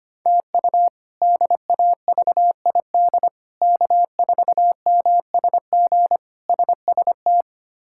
Morse code, heard as a keyed tone, about 25 wpm, 700 Hz. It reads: TU DAVID K4MSG SST
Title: K1USN SST QSO Practice - 25wpm